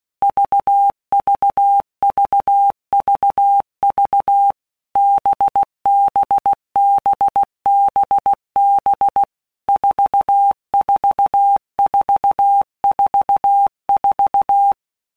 Morse code, heard as a keyed tone, about 16 wpm, 800 Hz. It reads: VVVVV BBBBB 44444
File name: Basic Letter Groups By WPM's / Groups@15 / VB46@15